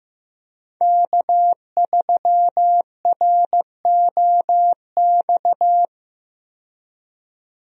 Morse code, heard as a keyed tone, about 15 wpm, 700 Hz. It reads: K3ROX
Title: USA Call Signs - 5 Characters - 15wpm